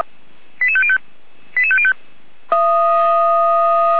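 Five-tone selective call (ZVEI), heard as quick runs of five tones, tones of about 70 ms. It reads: 88505 88505
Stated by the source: Five-tone ZVEI selective call received by SDR.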